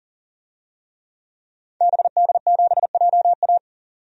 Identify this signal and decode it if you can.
Morse code, heard as a keyed tone, about 40 wpm, 700 Hz. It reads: BD7JA
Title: DXCC-100 Call Signs - 5 Characters - 40wpm